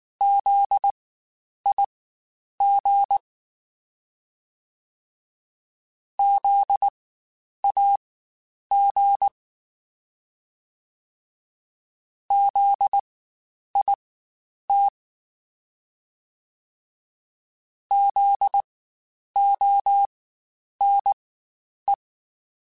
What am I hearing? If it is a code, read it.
Morse code, heard as a keyed tone, about 19 wpm, spaced as for 9 wpm, 800 Hz. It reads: ZIG ZAG ZIT ZONE
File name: Spaced Alphas / Spaced Alphas mp3s / Zs sp @18